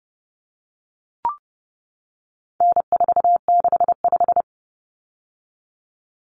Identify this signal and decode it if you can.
Morse code, heard as a keyed tone, about 30 wpm, 700 Hz. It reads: N465
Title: Common Cut Numbers - 4-Digit 30wpm